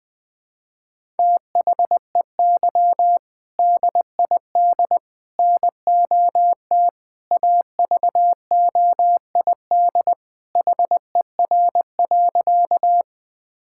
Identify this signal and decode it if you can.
Morse code, heard as a keyed tone, about 20 wpm, 700 Hz. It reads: THEY DID NOT AVOID HER.
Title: Sentences from Top 1000 Words - Review Part 6 - 20wpm